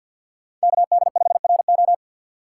Morse code, heard as a keyed tone, about 52 wpm, 690 Hz. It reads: KD5RO
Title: Speed Racing - USA 2x2 Call Signs 50wpm